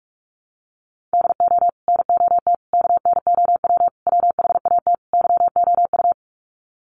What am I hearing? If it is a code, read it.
Morse code, heard as a keyed tone, about 44 wpm, 700 Hz. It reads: DO NOT KNOW WHAT YOU
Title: Sets of 5 Words 45wpm